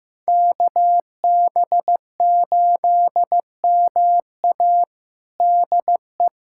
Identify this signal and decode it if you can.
Morse code, heard as a keyed tone, about 15 wpm, 700 Hz. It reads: KB8MA DE
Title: USA Callsign DE Callsign - 5 Characters - 15wpm